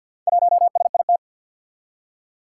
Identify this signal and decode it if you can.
Morse code, heard as a keyed tone, about 48 wpm, 700 Hz. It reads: 1SIT
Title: Four Challenging Letter-Number Combinations 50wpm